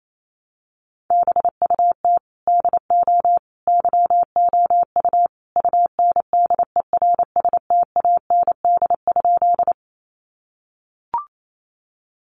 Morse code, heard as a keyed tone, about 28 wpm, 700 Hz. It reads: BUT DO YOU UNDERSTAND?
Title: Sentences from Top 600 Words 28wpm